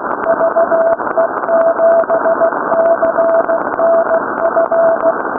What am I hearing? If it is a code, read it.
Morse code, heard as a keyed tone, about 16 wpm, 660 Hz, with noise. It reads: VE7CNF
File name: VCRLDE / VHS 3510